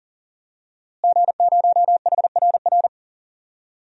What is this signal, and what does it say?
Morse code, heard as a keyed tone, about 40 wpm, 700 Hz. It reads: G0HRR
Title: DXCC-100 Call Signs - 5 Characters - Repeated 3x - 40wpm